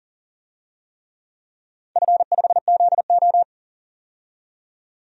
Morse code, heard as a keyed tone, about 37 wpm, 700 Hz. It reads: F5ZO
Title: DXCC-100 Call Signs - 4 Characters - 40wpm